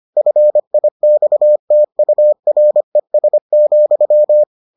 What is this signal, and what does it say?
Morse code, heard as a keyed tone, about 25 wpm, 600 Hz. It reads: FIXTURES,